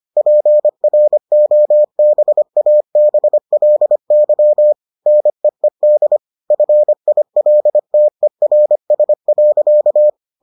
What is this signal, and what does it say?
Morse code, heard as a keyed tone, about 25 wpm, 600 Hz. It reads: PROBABLY NEED FILTERS.